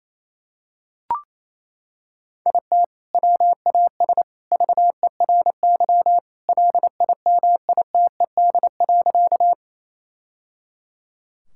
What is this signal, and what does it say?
Morse code, heard as a keyed tone, about 28 wpm, 700 Hz. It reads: IT WAS VERY LIMITED.
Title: Sentences from Top 1000 Words - Review Part 6 - 28wpm